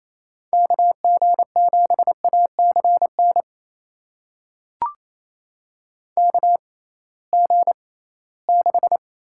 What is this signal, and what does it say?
Morse code, heard as a keyed tone, about 28 wpm, spaced as for 18 wpm, 700 Hz. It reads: KG7ACN KG6
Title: USA Call Signs - 6 Characters - 6x Character Spacing - 28wpm